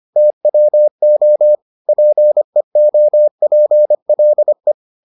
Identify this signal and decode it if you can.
Morse code, heard as a keyed tone, about 25 wpm, 600 Hz. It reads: TWO PEOPLE